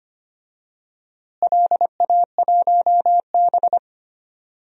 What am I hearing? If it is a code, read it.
Morse code, heard as a keyed tone, about 25 wpm, 700 Hz. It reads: LA1B
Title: DXCC-100 Call Signs - 4 Characters - Repeated 2x - 25wpm